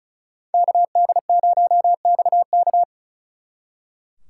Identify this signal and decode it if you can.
Morse code, heard as a keyed tone, about 46 wpm, 700 Hz. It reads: KD0XK